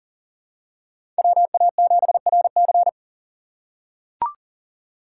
Morse code, heard as a keyed tone, about 40 wpm, 700 Hz. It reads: WA7RC
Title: USA Call Signs - 5 Characters - 3x Character Spacing - 40wpm